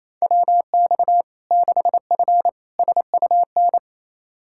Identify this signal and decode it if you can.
Morse code, heard as a keyed tone, about 27 wpm, 700 Hz. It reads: WX 6F SUN
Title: Practice QSO - Exchange 2 - Info - 28wpm